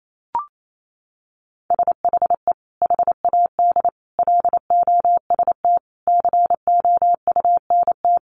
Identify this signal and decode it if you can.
Morse code, heard as a keyed tone, about 28 wpm, 700 Hz. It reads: SHE HAD LOST COUNT
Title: Sentences from Top 1500 Words 28wpm